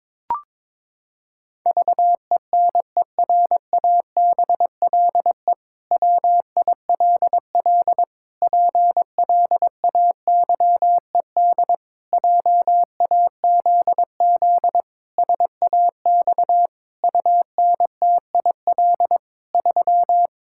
Morse code, heard as a keyed tone, about 22 wpm, 700 Hz. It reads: VENERABLE WILL PLAYED JAZZ SAX UNTIL 3
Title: Pangram Challenge v2 - 22wpm